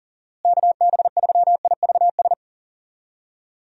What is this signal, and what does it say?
Morse code, heard as a keyed tone, about 40 wpm, 700 Hz. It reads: KB3IVS
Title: USA Call Signs - 6 Characters - Repeated 2x - 40wpm